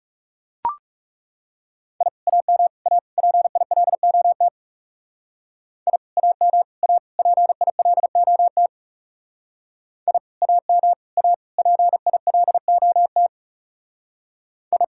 Morse code, heard as a keyed tone, about 36 wpm, 700 Hz. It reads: I AM A PILOT I AM A PILOT I AM A PILOT I
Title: Speed-Racing Common QSO Phrases - 30wpm